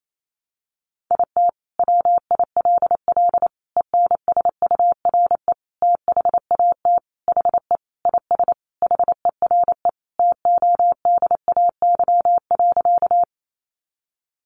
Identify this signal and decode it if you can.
Morse code, heard as a keyed tone, about 28 wpm, 700 Hz. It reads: IT WILL ENSURE THAT HE IS HERE TODAY.
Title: Sentences from Top 1100 Words 28wpm